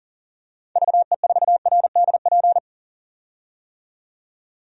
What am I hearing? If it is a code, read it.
Morse code, heard as a keyed tone, about 40 wpm, 700 Hz. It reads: VE4RDP
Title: DXCC-100 Call Signs - 6 Characters - 40wpm